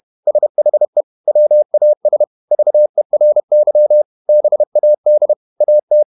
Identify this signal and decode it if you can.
Morse code, heard as a keyed tone, about 31 wpm, 600 Hz. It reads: SHE WAS VERY BAD AT
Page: Never 31wpm | CW med Gnister